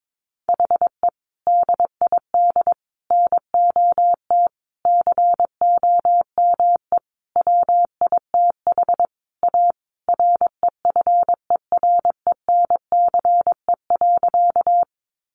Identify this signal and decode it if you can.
Morse code, heard as a keyed tone, about 22 wpm, 700 Hz. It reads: HE DID NOT COME WITH A REFERENCE.
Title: Sentences from Top 1200 Words 22wpm